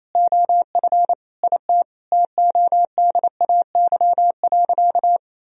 Morse code, heard as a keyed tone, about 28 wpm, 700 Hz. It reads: OF IT TODAY.
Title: Sentences from Top 1000 Words - Review Part 4 - 28wpm